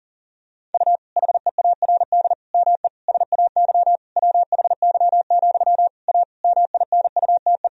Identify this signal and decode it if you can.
Morse code, heard as a keyed tone, about 40 wpm, 700 Hz. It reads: U HEARD ME SAY WHY, A MINUTE
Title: Sentences from Top 500 Words and Abbreviations 40wpm